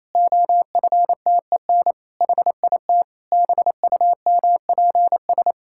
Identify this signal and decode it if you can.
Morse code, heard as a keyed tone, about 28 wpm, 700 Hz. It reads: OFTEN HIT BUMPS
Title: Pangram Challenge v2 - 28wpm